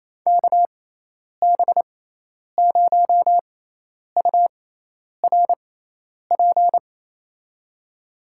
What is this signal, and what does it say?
Morse code, heard as a keyed tone, about 28 wpm, spaced as for 10 wpm, 700 Hz. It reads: KB0URP
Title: USA Call Signs - 6 Characters - 6x Character Spacing - 28wpm